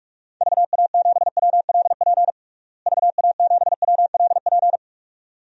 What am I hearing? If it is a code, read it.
Morse code, heard as a keyed tone, about 42 wpm, 700 Hz. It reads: VA7WLP VA7WLP
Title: DXCC-100 Call Signs - 6 Characters - Repeated 2x - 45wpm